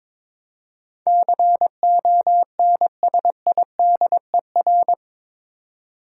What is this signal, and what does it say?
Morse code, heard as a keyed tone, about 22 wpm, 700 Hz. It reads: CONSIDER
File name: practice / 22wpm / Eight Letter Words - 22wpm